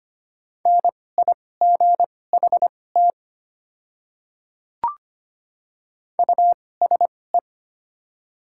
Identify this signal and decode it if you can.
Morse code, heard as a keyed tone, about 25 wpm, spaced as for 18 wpm, 700 Hz. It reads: NIGHT USE
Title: 2x Character Spacing - Top 300 Words - 25wpm